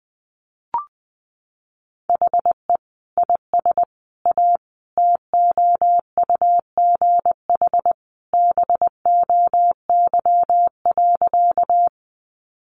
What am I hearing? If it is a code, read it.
Morse code, heard as a keyed tone, about 20 wpm, 700 Hz. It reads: HE IS A TOUGH BOY.
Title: Sentences from Top 1200 Words 20wpm